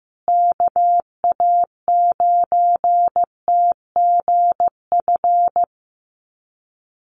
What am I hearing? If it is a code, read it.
Morse code, heard as a keyed tone, about 15 wpm, 700 Hz. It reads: KA9TGF
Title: USA Call Signs Repeated 2x 15wpm